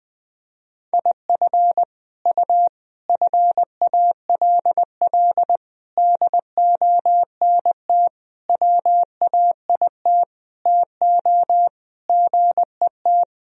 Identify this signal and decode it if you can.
Morse code, heard as a keyed tone, about 20 wpm, 700 Hz. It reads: IF U FALL DONT WAIT TO GET
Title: Sentences from Top 500 Words and Abbreviations 20wpm